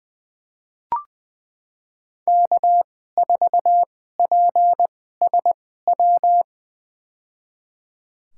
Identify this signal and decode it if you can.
Morse code, heard as a keyed tone, about 20 wpm, spaced as for 14 wpm, 700 Hz. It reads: K4PSW
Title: USA Call Signs - 2x Character Spacing - 20wpm